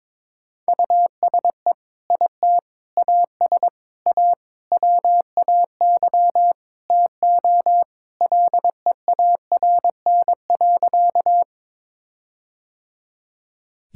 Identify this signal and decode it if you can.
Morse code, heard as a keyed tone, about 22 wpm, 700 Hz. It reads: USE IT AS A WAY TO LEARN.
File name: practice / 22wpm / Sentences with 4-Word Phrases - 22wpm